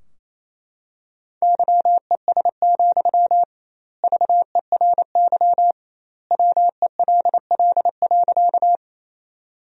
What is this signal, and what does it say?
Morse code, heard as a keyed tone, about 28 wpm, 700 Hz. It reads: YES, VERY WELL.
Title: 2x Word-Spacing - Sentences from Top 500 Words - 28wpm